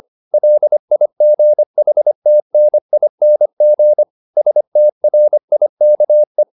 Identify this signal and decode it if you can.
Morse code, heard as a keyed tone, about 25 wpm, 600 Hz. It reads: LIGHTNING STRIKE